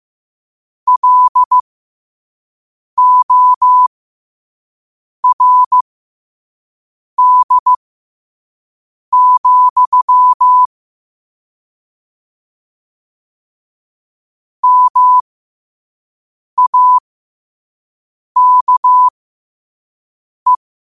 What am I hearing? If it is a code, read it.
Morse code, heard as a keyed tone, about 15 wpm, spaced as for 5 wpm, 1000 Hz. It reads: LORD, MAKE
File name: qotdfiles / 5wpm